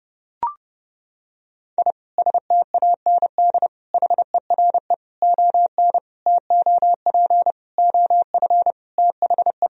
Morse code, heard as a keyed tone, about 32 wpm, 700 Hz. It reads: I STAND HERE ON TOP OF THE
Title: Sentences from Top 1000 Words - Review Part 1 - 30wpm